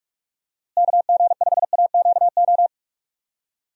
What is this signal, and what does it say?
Morse code, heard as a keyed tone, about 46 wpm, 700 Hz. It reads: KG5AQO
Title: USA Call Signs - 6 Characters - Repeated 3x - 45wpm